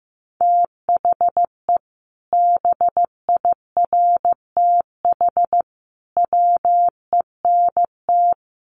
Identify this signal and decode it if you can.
Morse code, heard as a keyed tone, about 14 wpm, 700 Hz. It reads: THE BIRTH WENT
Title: Sentences from Top 800 Words 15wpm